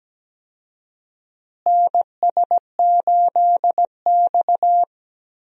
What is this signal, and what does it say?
Morse code, heard as a keyed tone, about 17 wpm, 700 Hz. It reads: NS8X
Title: USA Call Signs - 4 Characters - Repeated 2x - 17wpm